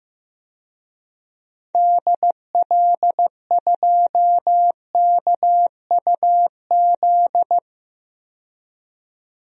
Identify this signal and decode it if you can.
Morse code, heard as a keyed tone, about 15 wpm, 700 Hz. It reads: DL2KUZ
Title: DXCC-100 Call Signs - 6 Characters - 15wpm